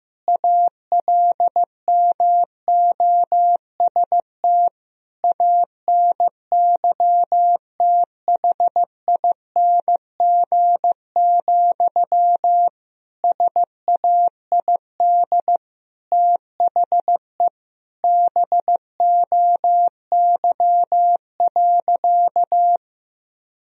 Morse code, heard as a keyed tone, about 15 wpm, 700 Hz. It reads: ALMOST ANYTHING, SAID THE BOY.
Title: Sentences from Top 1000 Words - Review Part 4 - 15wpm